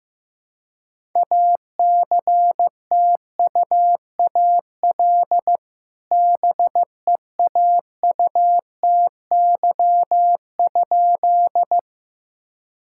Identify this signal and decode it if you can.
Morse code, heard as a keyed tone, about 15 wpm, 700 Hz. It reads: ACTUAL BEAUTY?
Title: Sentences from Top 1000 Words - Review Part 5 - 15wpm